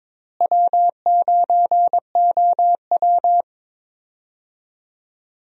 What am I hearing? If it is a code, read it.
Morse code, heard as a keyed tone, about 22 wpm, 700 Hz. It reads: W9OW
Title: USA Call Signs - 4 Characters - Repeated 2x - 22wpm